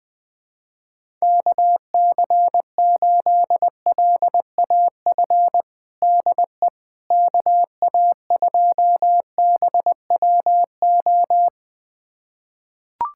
Morse code, heard as a keyed tone, about 20 wpm, 700 Hz. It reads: KC8LAF DE KA2BWO
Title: USA Callsign DE Callsign - 6 Characters - 20wpm